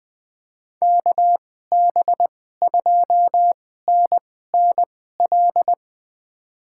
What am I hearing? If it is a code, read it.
Morse code, heard as a keyed tone, about 21 wpm, spaced as for 15 wpm, 700 Hz. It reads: KB2NNL
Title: USA Call Signs - 6 Characters - 2x Character Spacing - 20wpm